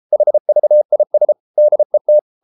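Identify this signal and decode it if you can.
Morse code, heard as a keyed tone, about 33 wpm, 600 Hz. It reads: HVIS DET
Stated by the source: Morse code at 33 wpm, QRQ.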